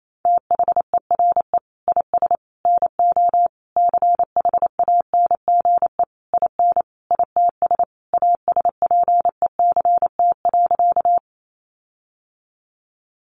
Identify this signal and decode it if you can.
Morse code, heard as a keyed tone, about 28 wpm, 700 Hz. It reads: THERE IS NO CHANGE IN ITS ASPECT.
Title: Sentences from Top 1300 Words 28wpm